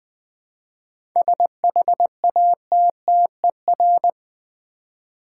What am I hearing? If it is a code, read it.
Morse code, heard as a keyed tone, about 20 wpm, 700 Hz. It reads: SHATTER
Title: 7 Letter Words - First 3 Letters Given - 20wpm